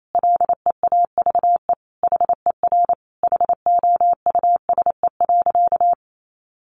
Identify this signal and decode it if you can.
Morse code, heard as a keyed tone, about 28 wpm, 700 Hz. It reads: LEAVE HER HOUSE.
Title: Sentences from Top 1500 Words 28wpm